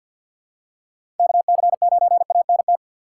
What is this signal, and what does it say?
Morse code, heard as a keyed tone, about 48 wpm, 700 Hz. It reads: KC9ANT
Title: USA Call Signs - 6 Characters - 4x Character Spacing - 50wpm